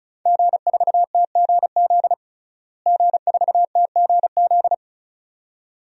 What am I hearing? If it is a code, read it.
Morse code, heard as a keyed tone, about 34 wpm, 700 Hz. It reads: G4TGZ G4TGZ
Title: DXCC-100 Call Signs - 5 Characters - Repeated 2x - 35wpm